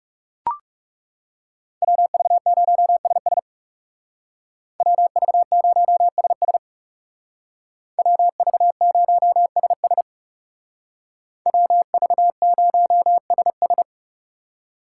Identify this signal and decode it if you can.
Morse code, heard as a keyed tone, about 39 wpm, 700 Hz. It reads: WV0SS WV0SS WV0SS WV0SS